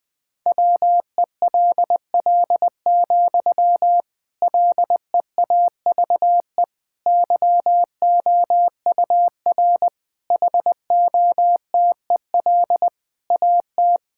Morse code, heard as a keyed tone, about 20 wpm, 700 Hz. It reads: WELL, LEAVE YOUR HOTEL AT
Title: Sentences from Top 1600 Words 20wpm